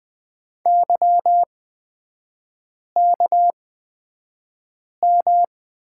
Morse code, heard as a keyed tone, about 20 wpm, spaced as for 5 wpm, 690 Hz. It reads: YKM
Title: 236 Lesson 38e - Letter q - Three Characters 20wpm